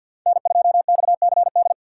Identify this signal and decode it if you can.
Morse code, heard as a keyed tone, about 47 wpm, 700 Hz. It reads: N2XCD
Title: USA Call Signs - 5 Characters - 3x Character Spacing - 50wpm